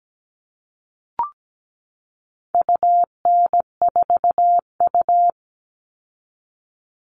Morse code, heard as a keyed tone, about 17 wpm, 700 Hz. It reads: UN4U
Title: All Cut Numbers - 4-Digit 17wpm